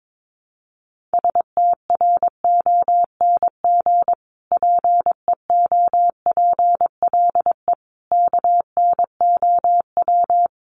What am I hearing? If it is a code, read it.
Morse code, heard as a keyed tone, about 22 wpm, 700 Hz. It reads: STRONG PEOPLE KNOW
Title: Sentences from Top 1500 Words 22wpm